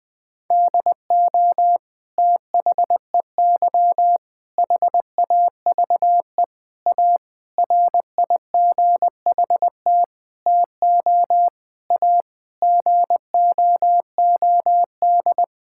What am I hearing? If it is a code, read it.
Morse code, heard as a keyed tone, about 20 wpm, 700 Hz. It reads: DO THEY HAVE A RIGHT TO A GOOD
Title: Sentences from Top 1000 Words - Review Part 6 - 20wpm